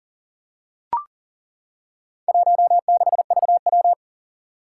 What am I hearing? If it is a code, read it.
Morse code, heard as a keyed tone, about 42 wpm, 700 Hz. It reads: 16VW
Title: Four Challenging Letter-Number Combinations 40wpm